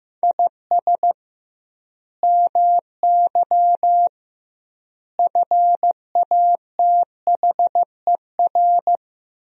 Morse code, heard as a keyed tone, about 15 wpm, 700 Hz. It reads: IS MY FATHER
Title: 2x Word-Spacing - Sentences from Top 300 Words - 15wpm